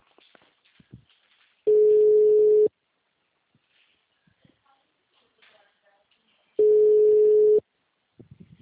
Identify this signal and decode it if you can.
Ringback tone (Europe, 425 Hz), with speech.
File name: out- -2023